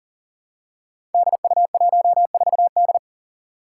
Morse code, heard as a keyed tone, about 43 wpm, 700 Hz. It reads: DU14D
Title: Five Challenging Letter-Number Combinations 40wpm